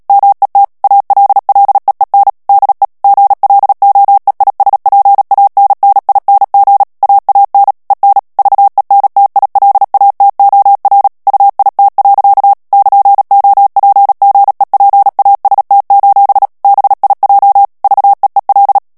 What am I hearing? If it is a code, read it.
Morse code, heard as a keyed tone, about 37 wpm, 800 Hz. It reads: MET ALLEEN DE GLOEISPANNING AAN EN VENTILATOR UIT. <KN>OPGEPAST: BIJ VEEL
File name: morse mp3 / HOLLANDAIS texte clair 2 vitesse 35 mots minute